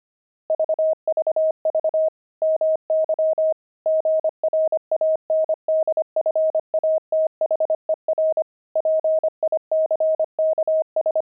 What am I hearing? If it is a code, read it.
Morse code, heard as a keyed tone, about 25 wpm, 630 Hz. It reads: VVV MY GRANDFATHER PICKS
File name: pangram-fs-25-cs-25-grandpa